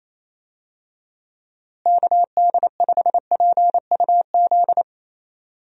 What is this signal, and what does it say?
Morse code, heard as a keyed tone, about 27 wpm, 700 Hz. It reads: KD5PUZ
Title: USA Call Signs - 6 Characters - Repeated 2x - 28wpm